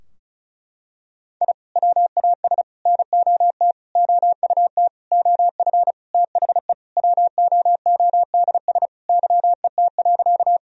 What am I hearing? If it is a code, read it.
Morse code, heard as a keyed tone, about 36 wpm, 700 Hz. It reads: I WAS NOT OUT OF THE WOODS YET.
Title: Sentences from Top 1000 Words - Review Part 2 - 35wpm